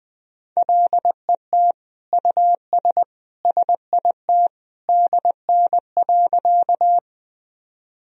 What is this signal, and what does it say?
Morse code, heard as a keyed tone, about 20 wpm, 700 Hz. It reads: LET US SIT DN.
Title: Sentences from Top 500 Words and Abbreviations 20wpm